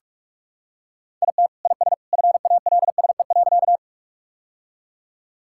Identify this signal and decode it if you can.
Morse code, heard as a keyed tone, about 46 wpm, 700 Hz. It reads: IT IS FALSE.
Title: Sentences from Top 1600 Words 45wpm